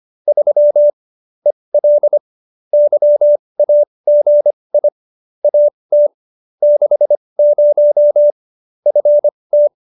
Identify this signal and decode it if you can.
Morse code, heard as a keyed tone, about 25 wpm, spaced as for 20 wpm, 590 Hz. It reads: 3 EL YAGI AT 60 FT
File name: QSO301 20